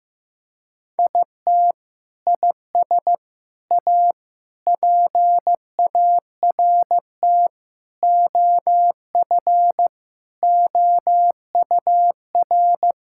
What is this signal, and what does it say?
Morse code, heard as a keyed tone, about 16 wpm, 700 Hz. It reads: IT IS A PART OF OUR
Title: Sentences from Top 1000 Words - Review Part 5 - 15wpm